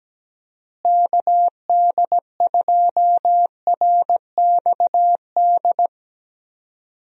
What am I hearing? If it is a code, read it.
Morse code, heard as a keyed tone, about 17 wpm, 700 Hz. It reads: KD2RXD
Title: USA Call Signs - 6 Characters - Repeated 2x - 17wpm